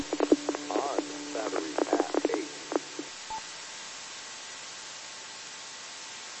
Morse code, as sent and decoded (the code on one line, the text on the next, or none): none